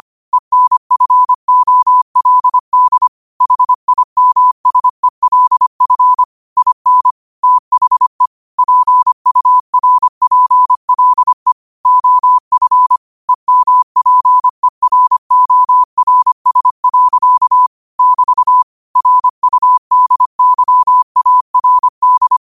Morse code, heard as a keyed tone, about 25 wpm, 1000 Hz. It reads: ENFOLD HIMSELF IN THE PURPLE OF EMPERORS. <BT> RUDYARD
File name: qotdfiles / 25wpm